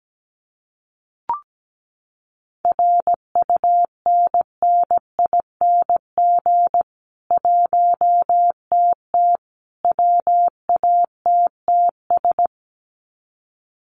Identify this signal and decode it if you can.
Morse code, heard as a keyed tone, about 17 wpm, 700 Hz. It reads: RUNNING 1TT WATTS
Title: Common QSO Phrases 17wpm